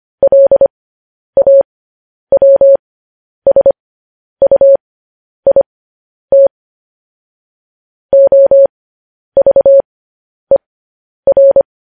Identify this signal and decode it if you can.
Morse code, heard as a keyed tone, about 25 wpm, spaced as for 10 wpm, 550 Hz. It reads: LAWSUIT OVER
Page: news-headlines-in-morse-code-at-10-wpm